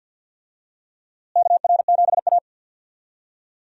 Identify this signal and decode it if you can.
Morse code, heard as a keyed tone, about 52 wpm, 700 Hz. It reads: KR7A
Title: USA Call Signs - 4 Characters - Repeated 2x - 50wpm